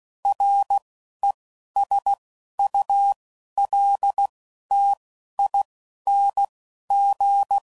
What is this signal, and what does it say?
Morse code, heard as a keyed tone, about 16 wpm, spaced as for 11 wpm, 800 Hz. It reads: RESULTING
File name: CW Code Groups @10WPM / Technical Subjects / XC